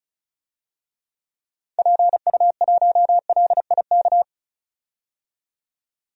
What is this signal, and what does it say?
Morse code, heard as a keyed tone, about 36 wpm, 700 Hz. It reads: PU1LIK
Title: DXCC-100 Call Signs - 6 Characters - 35wpm